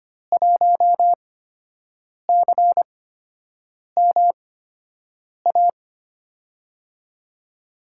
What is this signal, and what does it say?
Morse code, heard as a keyed tone, about 25 wpm, spaced as for 7 wpm, 700 Hz. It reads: 1CMA
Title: USA Call Signs - 5 Characters - 8x Character Spacing - 25wpm